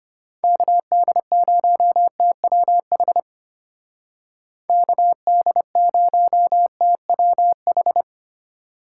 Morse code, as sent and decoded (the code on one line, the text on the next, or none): -.- -.. ----- - .-- .... / -.- -.. ----- - .-- ....
KD0TWH KD0TWH